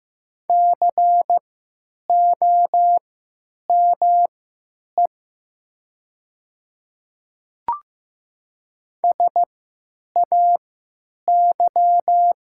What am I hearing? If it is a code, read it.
Morse code, heard as a keyed tone, about 15 wpm, spaced as for 8 wpm, 700 Hz. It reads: COME SAY
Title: 3x Character Spacing - Top 200 Words - 15wpm